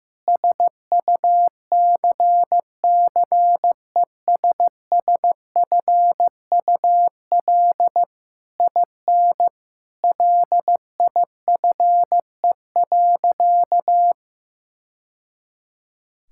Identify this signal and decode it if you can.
Morse code, heard as a keyed tone, about 15 wpm, 700 Hz. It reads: SUCCESSFUL IN LIFE.
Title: Sentences from Top 1000 Words - Review Part 6 - 15wpm